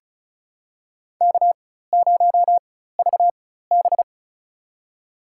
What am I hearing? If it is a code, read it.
Morse code, heard as a keyed tone, about 36 wpm, spaced as for 16 wpm, 700 Hz. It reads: K0VB
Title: USA Call Signs - 4 Characters - 4x Character Spacing - 35wpm